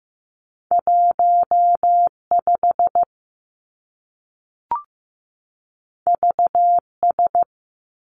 Morse code, heard as a keyed tone, about 15 wpm, 700 Hz. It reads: 1 5 V S
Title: Two Challenging Letter-Number Combinations 15wpm